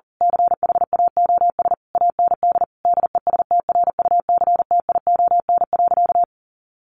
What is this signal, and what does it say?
Morse code, heard as a keyed tone, about 40 wpm, 700 Hz. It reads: CHAOS AND DESTRUCTION.
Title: A Sound Of Thunder 40wpm